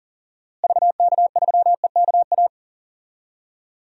Morse code, heard as a keyed tone, about 40 wpm, 700 Hz. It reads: VK3EKA
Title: DXCC-100 Call Signs - 6 Characters - Repeated 2x - 40wpm